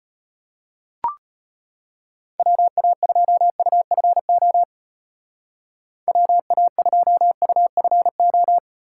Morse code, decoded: WA2UFO WA2UFO

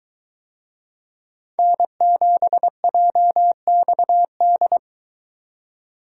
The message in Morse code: N7JXD

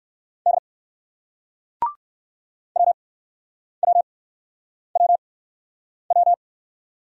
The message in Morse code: D WWWW

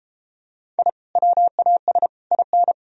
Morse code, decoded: I WAS IN

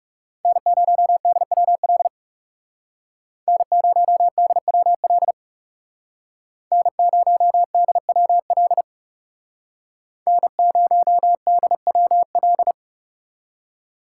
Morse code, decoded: N0DWL N0DWL N0DWL N0DWL

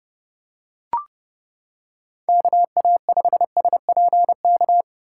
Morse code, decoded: KA5SPK

